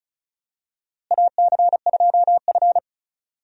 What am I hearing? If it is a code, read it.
Morse code, heard as a keyed tone, about 36 wpm, 700 Hz. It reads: AC2F
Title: USA Call Signs - 4 Characters - Repeated 3x - 35wpm